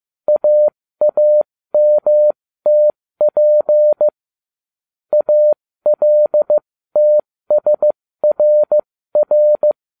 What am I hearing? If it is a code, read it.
Morse code, heard as a keyed tone, about 15 wpm, spaced as for 13 wpm, 600 Hz. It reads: AAMTP ALTSRR